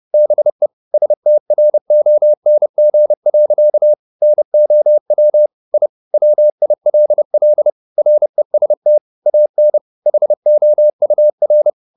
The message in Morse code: BE STRONG. NOW I WILL REST AN HOUR

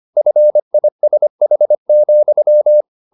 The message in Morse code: FISH,

